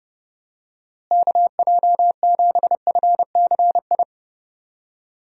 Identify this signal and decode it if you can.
Morse code, heard as a keyed tone, about 30 wpm, 700 Hz. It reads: KJ7FCI
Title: USA Call Signs - 6 Characters - Repeated 3x - 30wpm